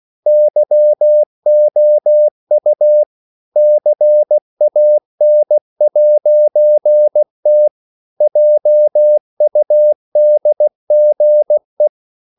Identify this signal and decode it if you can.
Morse code, heard as a keyed tone, about 16 wpm, 600 Hz. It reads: YOU CAN'T JUDGE
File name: CW / Practice / EnglishSayings16